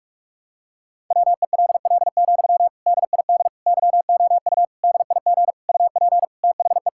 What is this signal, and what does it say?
Morse code, heard as a keyed tone, about 46 wpm, 700 Hz. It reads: WELL, DID YOU DIG UP THE
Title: Sentences from Top 1600 Words 45wpm